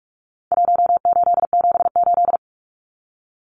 Morse code, heard as a keyed tone, about 44 wpm, 700 Hz. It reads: 1878
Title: Numbers - Contest Style - 4-Digit 45wpm